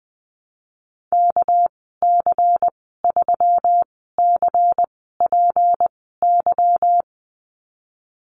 Morse code, heard as a keyed tone, about 20 wpm, spaced as for 15 wpm, 700 Hz. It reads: KC3CPY
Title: 207 Lesson 33g - Letter j - Call Signs 20wpm